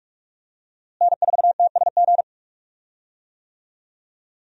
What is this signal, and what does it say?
Morse code, heard as a keyed tone, about 46 wpm, 700 Hz. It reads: N4TSG
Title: USA Call Signs - 5 Characters - Repeated 3x - 45wpm